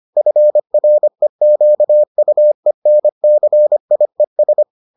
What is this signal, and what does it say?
Morse code, heard as a keyed tone, about 25 wpm, 600 Hz. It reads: FREQUENCIES